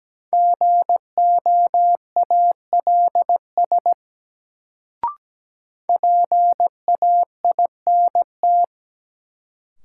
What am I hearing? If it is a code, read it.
Morse code, heard as a keyed tone, about 17 wpm, 700 Hz. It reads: GOALS PAINT
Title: Sentences from Top 1000 Words - Review Part 6 - New Word Drill - 17wpm